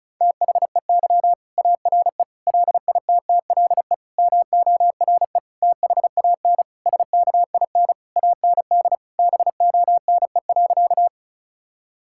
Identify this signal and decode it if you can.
Morse code, heard as a keyed tone, about 36 wpm, 700 Hz. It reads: THEY ARE LITTLE MORE THAN SKIN AND BONE.
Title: Sentences from Top 1600 Words 35wpm